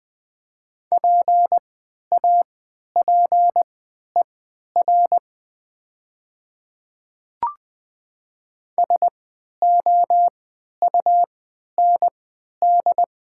Morse code, heard as a keyed tone, about 20 wpm, spaced as for 11 wpm, 700 Hz. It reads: PAPER SOUND
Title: 3x Character Spacing - Top 500 Words - 20wpm